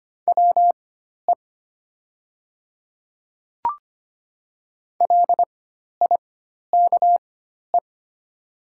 Morse code, heard as a keyed tone, about 25 wpm, spaced as for 12 wpm, 690 Hz. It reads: WE LIKE
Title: 4x Character Spacing - Top 200 Words - 25wpm